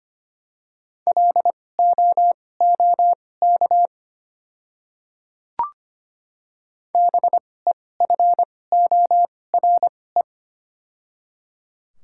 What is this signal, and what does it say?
Morse code, heard as a keyed tone, about 25 wpm, spaced as for 18 wpm, 690 Hz. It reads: LOOK BEFORE
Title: 2x Character Spacing - Top 300 Words - 25wpm